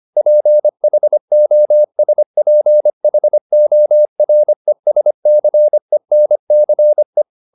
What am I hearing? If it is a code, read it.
Morse code, heard as a keyed tone, about 25 wpm, 600 Hz. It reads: PHOSPHORESCENCE
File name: the old man and the sea part 1 25wpm